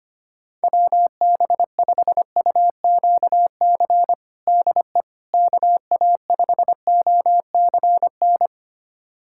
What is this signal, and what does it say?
Morse code, heard as a keyed tone, about 25 wpm, 700 Hz. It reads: WB5UQC DE KA5OCN
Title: USA Callsign DE Callsign - 6 Characters - 25wpm